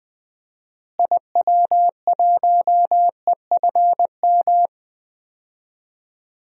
Morse code, heard as a keyed tone, about 20 wpm, 700 Hz. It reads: IW1EFM
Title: DXCC-100 Call Signs - 6 Characters - Repeated 2x - 20wpm